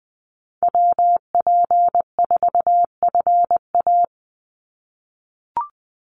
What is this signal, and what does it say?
Morse code, heard as a keyed tone, about 20 wpm, 700 Hz. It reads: WP4FA
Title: USA Call Signs 20wpm